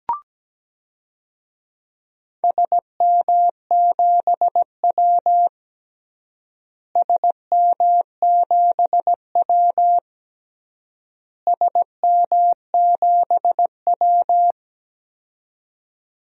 Morse code, heard as a keyed tone, about 17 wpm, 700 Hz. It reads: SM7W SM7W SM7W
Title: DXCC-100 Call Signs - 4 Characters - Repeated 3x - 17wpm